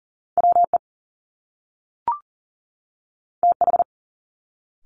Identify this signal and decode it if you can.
Morse code, heard as a keyed tone, about 38 wpm, 700 Hz. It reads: WE TH
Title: Top Two Letter Combinations in Words 40wpm